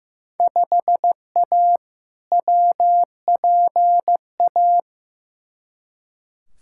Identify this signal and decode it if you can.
Morse code, heard as a keyed tone, about 15 wpm, 700 Hz. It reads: 5A WPA
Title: Field Day Exchanges Encore - 15wpm